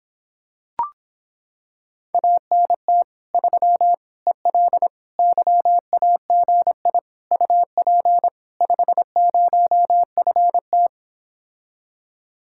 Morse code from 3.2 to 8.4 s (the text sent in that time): 3 EL YAGI UP